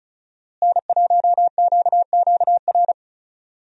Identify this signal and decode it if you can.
Morse code, heard as a keyed tone, about 34 wpm, 700 Hz. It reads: N1QQR
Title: USA Call Signs - 5 Characters - Repeated 2x - 35wpm